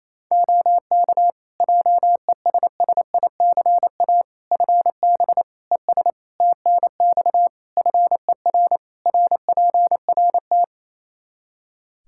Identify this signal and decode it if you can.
Morse code, heard as a keyed tone, about 28 wpm, 700 Hz. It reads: OK JESSICA FB ES TNX FER RPRT
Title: Practice QSO - Exchange 2 - Info - 28wpm